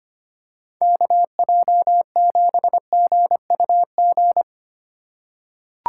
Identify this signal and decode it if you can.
Morse code, heard as a keyed tone, about 25 wpm, 700 Hz. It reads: KJ7GUG
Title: USA Call Signs - 6 Characters - 4x Character Spacing - 25wpm